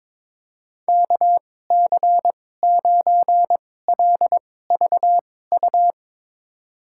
Morse code, heard as a keyed tone, about 22 wpm, spaced as for 16 wpm, 700 Hz. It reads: KC9LVU
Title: USA Call Signs - 6 Characters - 2x Character Spacing - 22wpm